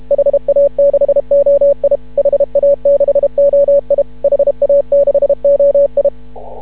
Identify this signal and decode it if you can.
Morse code, heard as a keyed tone, about 31 wpm, 560 Hz. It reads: HA6OI HA6OI HA6OI
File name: CW / ha6oi 2